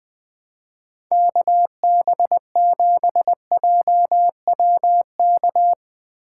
Morse code, heard as a keyed tone, about 21 wpm, 700 Hz. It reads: KB7JWK
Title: Speed Racing - USA 2x3 Call Signs 20wpm